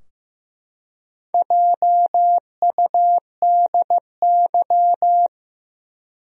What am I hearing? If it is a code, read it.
Morse code, heard as a keyed tone, about 15 wpm, 700 Hz. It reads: JUDY
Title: USA Common Names - Max 5 Letters - 15wpm